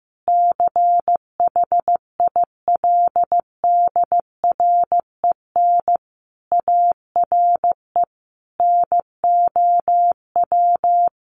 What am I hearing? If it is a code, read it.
Morse code, heard as a keyed tone, about 15 wpm, 700 Hz. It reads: CHILDREN ARE NOW
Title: Sentences from Top 1500 Words 15wpm